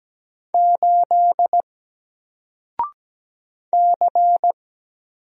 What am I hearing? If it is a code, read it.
Morse code, heard as a keyed tone, about 17 wpm, 700 Hz. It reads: 8 C
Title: Single Letter-Number 17wpm